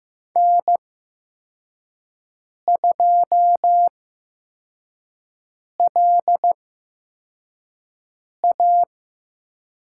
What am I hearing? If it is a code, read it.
Morse code, heard as a keyed tone, about 15 wpm, spaced as for 4 wpm, 700 Hz. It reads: N2LA
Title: USA Call Signs - 8x Character Spacing - 15wpm